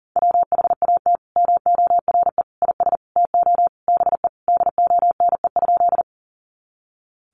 Morse code, heard as a keyed tone, about 40 wpm, 700 Hz. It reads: WHAT MORE IS TO BE DONE?
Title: Sentences from Top 500 Words 40wpm